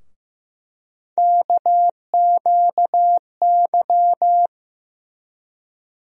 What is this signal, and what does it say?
Morse code, heard as a keyed tone, about 15 wpm, 700 Hz. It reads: KQY
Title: USA Call Sign Postfixes - 3 Letters 15wpm